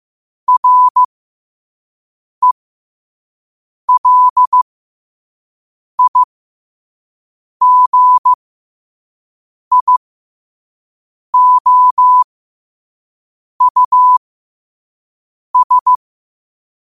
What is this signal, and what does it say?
Morse code, heard as a keyed tone, about 15 wpm, spaced as for 5 wpm, 1000 Hz. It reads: RELIGIOUS
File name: qotdfiles / 5wpm